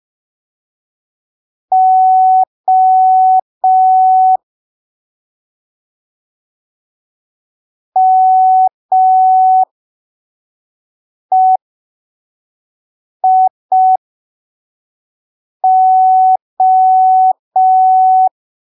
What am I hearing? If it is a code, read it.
Morse code, heard as a keyed tone, about 5 wpm, spaced as for 3 wpm, 750 Hz. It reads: O MEIO
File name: arquivos / texto / texto 10 05ppm